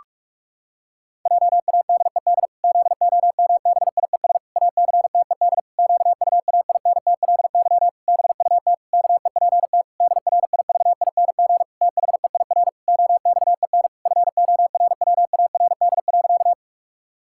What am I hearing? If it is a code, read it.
Morse code, heard as a keyed tone, about 46 wpm, 700 Hz. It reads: JADED ZOMBIES ACTED QUAINTLY BUT KEPT DRIVING THEIR OXEN FORWARD.
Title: Pangram Challenge v2 - 45wpm